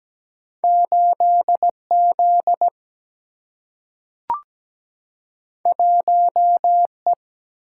Morse code, heard as a keyed tone, about 17 wpm, 700 Hz. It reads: 8Z 1E
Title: Single-Letter Single-Number 17wpm